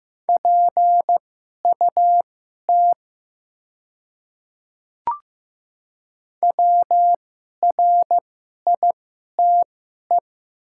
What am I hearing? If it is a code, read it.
Morse code, heard as a keyed tone, about 15 wpm, spaced as for 11 wpm, 700 Hz. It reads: PUT WRITE
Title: 2x Character Spacing - Top 500 Words - 15wpm